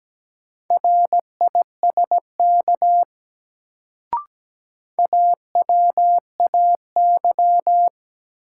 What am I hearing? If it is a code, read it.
Morse code, heard as a keyed tone, about 17 wpm, 700 Hz. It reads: RISK AWAY
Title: Sentences from Top 600 Words - New Word Drill - 17wpm